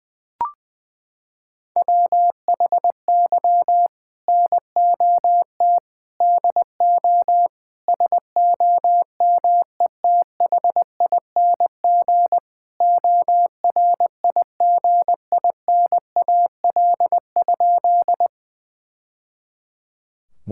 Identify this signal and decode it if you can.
Morse code, heard as a keyed tone, about 21 wpm, 700 Hz. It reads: WHY NOT DO SOMETHING ORIGINAL?
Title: Sentences from Top 1000 Words - Review Part 6 - 20wpm